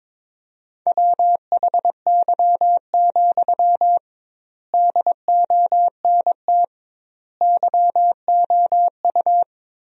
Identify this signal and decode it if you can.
Morse code, heard as a keyed tone, about 22 wpm, 700 Hz. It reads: WHY, DONT YOU
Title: 2x Word-Spacing - Sentences from Top 300 Words - 22wpm